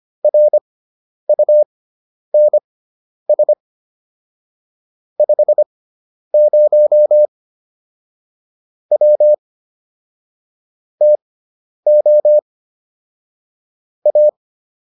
Morse code, heard as a keyed tone, about 25 wpm, spaced as for 10 wpm, 600 Hz. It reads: RUNS 50 W TO A
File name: QSO302 10